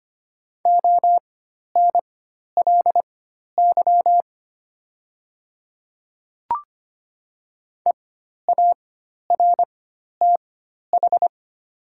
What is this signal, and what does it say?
Morse code, heard as a keyed tone, about 25 wpm, spaced as for 12 wpm, 700 Hz. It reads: ONLY EARTH
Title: 4x Character Spacing - Top 200 Words - 25wpm